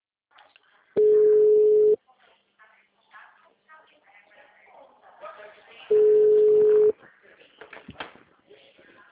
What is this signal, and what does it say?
Ringback tone (Europe, 425 Hz), with speech.
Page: Atibaia